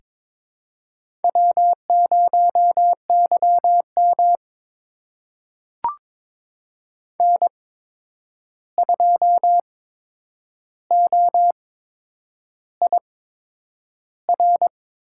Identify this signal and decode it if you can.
Morse code, heard as a keyed tone, about 22 wpm, spaced as for 9 wpm, 700 Hz. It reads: W0YM N2OIR
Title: USA Call Signs - 8x Character Spacing - 22wpm